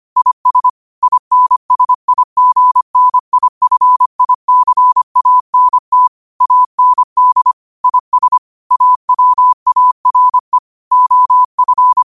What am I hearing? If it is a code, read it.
Morse code, heard as a keyed tone, about 25 wpm, 1000 Hz. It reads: IS INSIGNIFICANT AND IS AWARE OF
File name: 25wpm